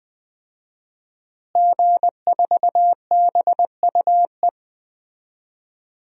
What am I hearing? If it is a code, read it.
Morse code, heard as a keyed tone, about 21 wpm, 700 Hz. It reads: G4BUE
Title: DXCC-100 Call Signs - 5 Characters - Repeated 3x - 20wpm